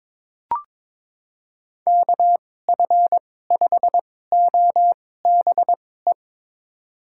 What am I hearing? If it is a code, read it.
Morse code, heard as a keyed tone, about 22 wpm, spaced as for 16 wpm, 700 Hz. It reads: KF5OBE
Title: USA Call Signs - 6 Characters - 2x Character Spacing - 22wpm